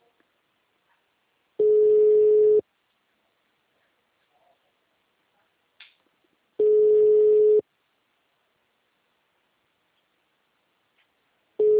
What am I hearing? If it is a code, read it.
Ringback tone (Europe, 425 Hz).
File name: out- -2010- - - .282